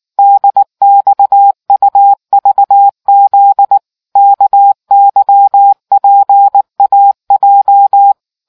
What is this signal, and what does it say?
Morse code, heard as a keyed tone, about 19 wpm, 800 Hz. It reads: DXUVZ KYPAJ